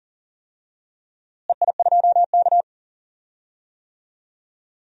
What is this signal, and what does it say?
Morse code, heard as a keyed tone, about 40 wpm, 700 Hz. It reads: EI2K
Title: DXCC-100 Call Signs - 4 Characters - Repeated 3x - 40wpm